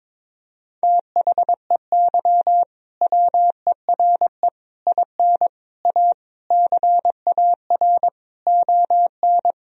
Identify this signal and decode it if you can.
Morse code, heard as a keyed tone, about 22 wpm, 700 Hz. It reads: THEY WERE IN A CAR ON
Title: Sentences from Top 1000 Words - Review Part 1 - 22wpm